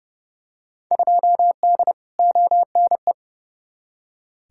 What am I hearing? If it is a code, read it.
Morse code, heard as a keyed tone, about 32 wpm, 700 Hz. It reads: 2D ONE
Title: Field Day Exchanges - 30wpm